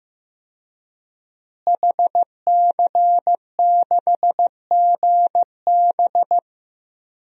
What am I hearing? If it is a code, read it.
Morse code, heard as a keyed tone, about 15 wpm, 700 Hz. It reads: HC6GB
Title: DXCC-100 Call Signs - 5 Characters - Repeated 2x - 15wpm